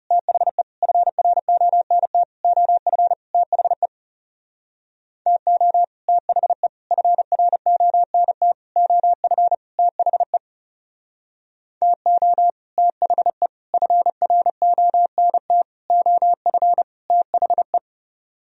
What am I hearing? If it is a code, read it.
Morse code, heard as a keyed tone, about 36 wpm, 700 Hz. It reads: THE FRONT OF THE TO THE FRONT OF THE TO THE FRONT OF THE
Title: Speed-Racing - Sets of 5 Words 30wpm